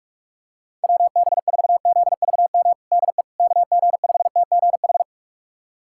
Morse code, heard as a keyed tone, about 46 wpm, 700 Hz. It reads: WB4ZVM DE KG5TGH
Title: USA Callsign DE Callsign - 6 Characters - 45wpm